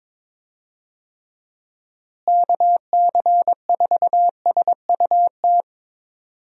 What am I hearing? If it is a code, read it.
Morse code, heard as a keyed tone, about 22 wpm, 700 Hz. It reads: KC4SUT
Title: USA Call Signs - 6 Characters - Repeated 2x - 22wpm